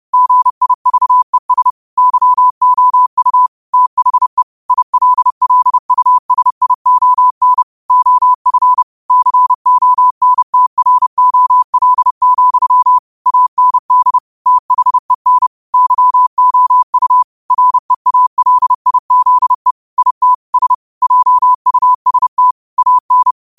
Morse code, heard as a keyed tone, about 30 wpm, 1000 Hz. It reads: GIVES YOU THE ILLUSION OF CONTROL, AND THEN YOU REALIZE IT S JUST AN
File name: qotdfiles / 30wpm